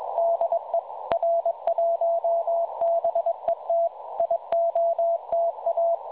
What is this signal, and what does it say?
Morse code, heard as a keyed tone, about 21 wpm, 670 Hz. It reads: DE R1BET IOTA
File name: websdr recording 7013.5kHz